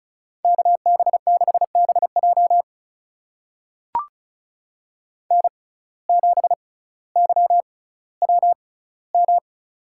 Morse code, heard as a keyed tone, about 34 wpm, spaced as for 18 wpm, 700 Hz. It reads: KB6BJ N7YWM
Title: USA Call Signs - 6x Character Spacing - 35wpm